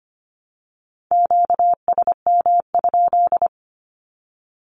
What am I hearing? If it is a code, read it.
Morse code, heard as a keyed tone, about 25 wpm, 700 Hz. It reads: QSM?
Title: Q-Codes 25wpm